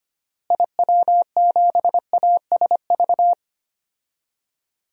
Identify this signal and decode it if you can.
Morse code, heard as a keyed tone, about 25 wpm, 700 Hz. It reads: IW7ASV
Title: DXCC-100 Call Signs - 6 Characters - Repeated 2x - 25wpm